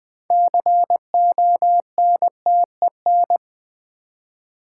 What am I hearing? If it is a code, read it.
Morse code, heard as a keyed tone, about 20 wpm, 700 Hz. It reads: CONTEN
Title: Build A Word - 3 Letters At A Time - 9-Letter Words - 20wpm